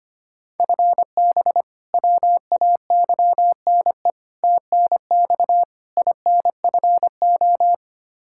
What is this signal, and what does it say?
Morse code, heard as a keyed tone, about 25 wpm, 700 Hz. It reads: FB WAYNE TNX INFO
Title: K1USN SST QSO Practice 25wpm